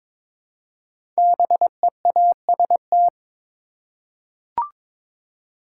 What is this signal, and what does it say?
Morse code, heard as a keyed tone, about 22 wpm, 700 Hz. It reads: BEAST
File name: 5 Letter Words - First 3 Letters Given - 22wpm